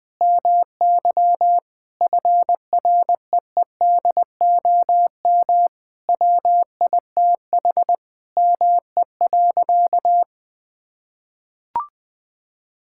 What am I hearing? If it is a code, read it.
Morse code, heard as a keyed tone, about 20 wpm, 700 Hz. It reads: MY FREEDOM WITH ME.
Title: Sentences from Top 1000 Words - Review Part 5 - 20wpm